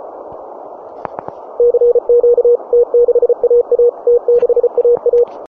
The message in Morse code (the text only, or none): CQ T6AA T6AA